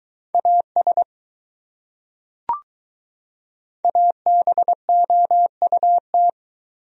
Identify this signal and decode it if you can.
Morse code, heard as a keyed tone, about 20 wpm, 700 Hz. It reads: AS ABOUT